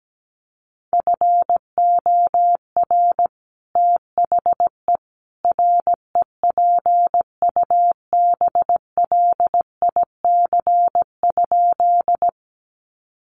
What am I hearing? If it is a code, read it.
Morse code, heard as a keyed tone, about 17 wpm, 700 Hz. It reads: FOR THE REPUBLIC?
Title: Sentences from Top 1500 Words 17wpm